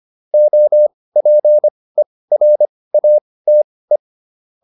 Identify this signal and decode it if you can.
Morse code, heard as a keyed tone, about 25 wpm, spaced as for 18 wpm, 600 Hz. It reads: OPERATE